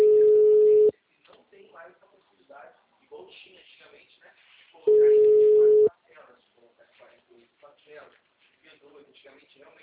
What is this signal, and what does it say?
Ringback tone (Europe, 425 Hz), with speech.
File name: out- -2016- - - .457